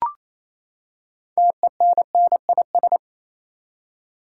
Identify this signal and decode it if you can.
Morse code, heard as a keyed tone, about 28 wpm, 700 Hz. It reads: TENNIS
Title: Six Letter Words - 28wpm